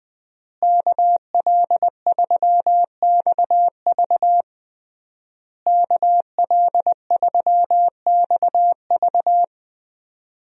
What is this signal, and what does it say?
Morse code, heard as a keyed tone, about 20 wpm, 700 Hz. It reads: KL3XV KL3XV
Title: USA Call Signs - 5 Characters - Repeated 3x - 20wpm